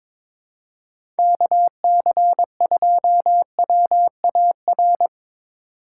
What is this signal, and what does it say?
Morse code, heard as a keyed tone, about 22 wpm, 700 Hz. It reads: KC2WAR
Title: USA Call Signs - 6 Characters - 2x Character Spacing - 22wpm